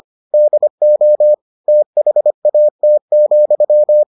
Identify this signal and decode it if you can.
Morse code, heard as a keyed tone, about 25 wpm, 600 Hz. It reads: DO THAT,